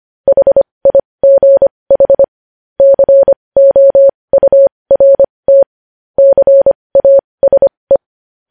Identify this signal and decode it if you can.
Morse code, heard as a keyed tone, about 25 wpm, spaced as for 20 wpm, 550 Hz. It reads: HIGH COURT CASE
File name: podcast / News headlines-1x-25x20